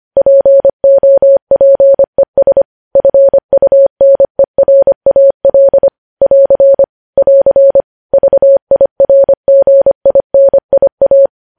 Morse code, heard as a keyed tone, about 25 wpm, 550 Hz. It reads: POPES FUNERAL <AR> <AR> VIRGINIA
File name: podcast / News headlines-1x-25x25